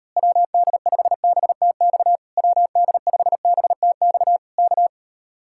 Morse code, heard as a keyed tone, about 39 wpm, 700 Hz. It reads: WD5BTX WD5BTX K